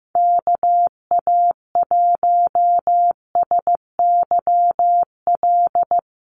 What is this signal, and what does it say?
Morse code, heard as a keyed tone, about 15 wpm, 700 Hz. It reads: KA1SYL
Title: USA Call Signs Repeated 3x 15wpm